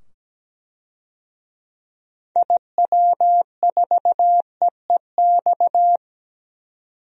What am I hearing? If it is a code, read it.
Morse code, heard as a keyed tone, about 17 wpm, 700 Hz. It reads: IW4EEX
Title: DXCC-100 Call Signs - 6 Characters - 17wpm